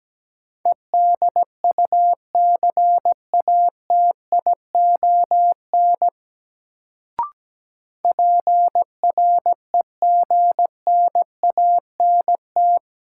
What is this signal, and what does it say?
Morse code, heard as a keyed tone, about 17 wpm, 700 Hz. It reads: EDUCATION PREGNANT
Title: Sentences from Top 1000 Words - Review Part 6 - New Word Drill - 17wpm